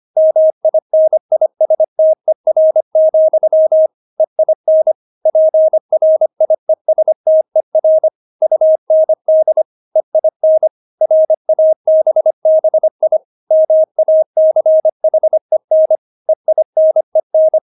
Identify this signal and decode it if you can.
Morse code, heard as a keyed tone, about 25 wpm, 630 Hz. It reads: MINISTER, EIN PRIESTER UND EIN RABBI MACHEN EINEN
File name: Witz6 25wpm